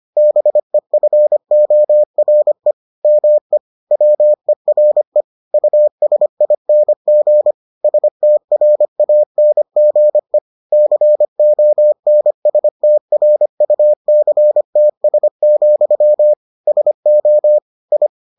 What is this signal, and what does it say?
Morse code, heard as a keyed tone, about 25 wpm, 600 Hz. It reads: BEFORE ME WERE USING STRANGE CONSTRUCTS, SO I